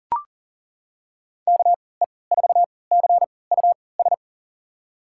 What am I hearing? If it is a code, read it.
Morse code, heard as a keyed tone, about 40 wpm, spaced as for 23 wpm, 700 Hz. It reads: KE4CUS
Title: USA Call Signs - 6 Characters - 3x Character Spacing - 40wpm